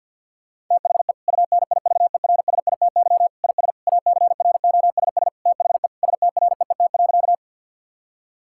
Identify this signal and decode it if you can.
Morse code, heard as a keyed tone, about 50 wpm, 700 Hz. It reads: THE UNIVERSITY IS ACROSS THE STREET.
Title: Sentences from Top 1000 Words - Review Part 5 - 50wpm